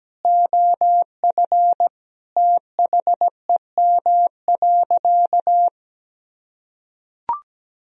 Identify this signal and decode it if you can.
Morse code, heard as a keyed tone, about 17 wpm, 700 Hz. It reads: OF THEM.
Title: Sentences from Top 1000 Words - Review Part 2 - 17wpm